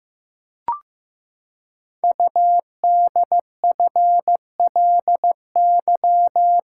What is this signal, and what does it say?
Morse code, heard as a keyed tone, about 15 wpm, 700 Hz. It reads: UDFLY
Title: Code Groups - 5 Letters - 15wpm